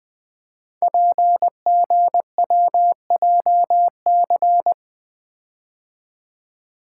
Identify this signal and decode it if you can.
Morse code, heard as a keyed tone, about 20 wpm, 700 Hz. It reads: PGWJC
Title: Code Groups - 5 Letters - 20wpm